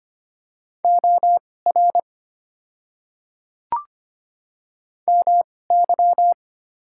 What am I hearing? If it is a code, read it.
Morse code, heard as a keyed tone, about 25 wpm, spaced as for 18 wpm, 690 Hz. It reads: OR MY
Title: 2x Character Spacing - Top 100 Words - 25wpm